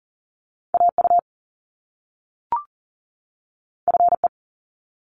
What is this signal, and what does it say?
Morse code, heard as a keyed tone, about 39 wpm, 700 Hz. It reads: AU FE
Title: Top Two Letter Combinations in Words 40wpm